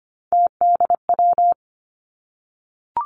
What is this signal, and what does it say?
Morse code, heard as a keyed tone, about 25 wpm, 700 Hz. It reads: TDW
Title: Three Challenging Letter-Number Combinations 25wpm